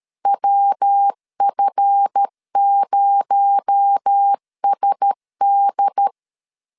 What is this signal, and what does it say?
Morse code, heard as a keyed tone, about 13 wpm, 800 Hz. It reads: WF0SD
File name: LEZIONI / 8 YJ45 / 8 YJ4 12wpm 2151